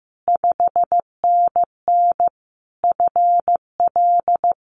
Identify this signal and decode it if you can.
Morse code, heard as a keyed tone, about 15 wpm, 700 Hz. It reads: 5NN FL
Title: Common QSO Phrases Encore 15wpm